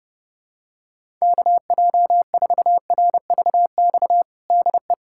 Morse code, heard as a keyed tone, about 32 wpm, 700 Hz. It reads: KJ4RVX DE
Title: USA Callsign DE Callsign - 6 Characters - 30wpm